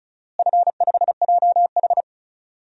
Morse code, heard as a keyed tone, about 34 wpm, 700 Hz. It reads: F5JH
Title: DXCC-100 Call Signs - 4 Characters - Repeated 2x - 35wpm